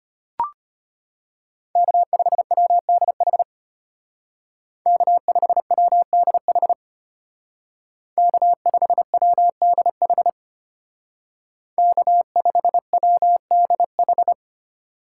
Morse code, heard as a keyed tone, about 33 wpm, 700 Hz. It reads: K5WDH K5WDH K5WDH K5WDH